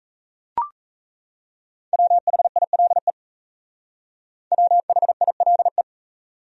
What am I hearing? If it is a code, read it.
Morse code, heard as a keyed tone, about 35 wpm, 700 Hz. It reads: WHILE WHILE
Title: Speed-Racing Top 500 Words 28wpm